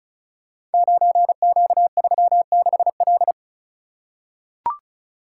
Morse code, heard as a keyed tone, about 36 wpm, 690 Hz. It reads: 9Q36L
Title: Code Groups - 5 Letter:Numbers - 35wpm